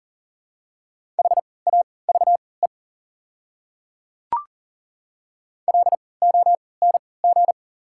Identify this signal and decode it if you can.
Morse code, heard as a keyed tone, about 40 wpm, spaced as for 23 wpm, 690 Hz. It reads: HAVE LONG
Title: 3x Character Spacing - Top 100 Words - 40wpm